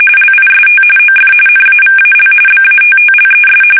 RTTY, about 45.45 baud. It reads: LIST OF NAVAL BATTLES
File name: fsk850 / usnavy3